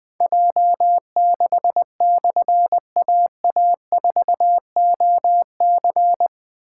Morse code, heard as a keyed tone, about 20 wpm, 700 Hz. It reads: J6/AA4OC
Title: Difficult Call Signs 20wpm